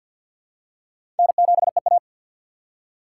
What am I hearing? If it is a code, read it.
Morse code, heard as a keyed tone, about 52 wpm, 700 Hz. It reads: N7EA
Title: USA Call Signs - 3x Character Spacing - 50wpm